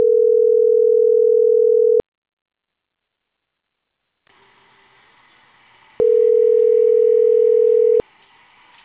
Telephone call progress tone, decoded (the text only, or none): ringback tone